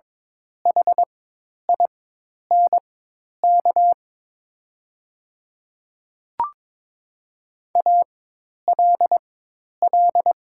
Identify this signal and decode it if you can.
Morse code, heard as a keyed tone, about 23 wpm, spaced as for 10 wpm, 700 Hz. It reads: HINK ALL
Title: 4x Character Spacing - Top 300 Words - 22wpm